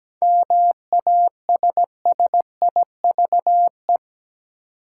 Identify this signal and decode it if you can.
Morse code, heard as a keyed tone, about 17 wpm, 700 Hz. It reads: MASSIVE
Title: Seven Letter Words - 17wpm